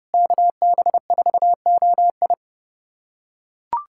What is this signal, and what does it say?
Morse code, heard as a keyed tone, about 30 wpm, 700 Hz. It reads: KB4OI
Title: USA Callsign DE Callsign - 5 Characters - 30wpm